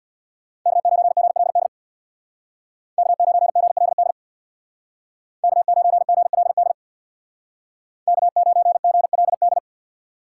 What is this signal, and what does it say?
Morse code, heard as a keyed tone, about 64 wpm, 700 Hz. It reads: K9GLD K9GLD K9GLD K9GLD